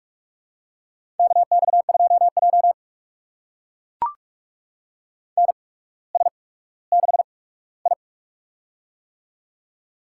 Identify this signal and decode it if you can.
Morse code, heard as a keyed tone, about 44 wpm, spaced as for 19 wpm, 700 Hz. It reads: KX2J NS6I
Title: USA Call Signs - 4 Characters - 8x Character Spacing - 45wpm